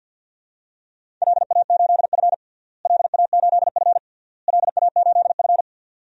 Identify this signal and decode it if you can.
Morse code, heard as a keyed tone, about 50 wpm, 700 Hz. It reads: LA8F LA8F LA8F
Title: DXCC-100 Call Signs - 4 Characters - Repeated 3x - 50wpm